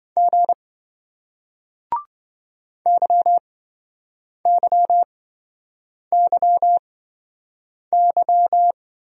Morse code, heard as a keyed tone, about 25 wpm, spaced as for 7 wpm, 700 Hz. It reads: G YYYY
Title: Speed-Racing Single Letters 20wpm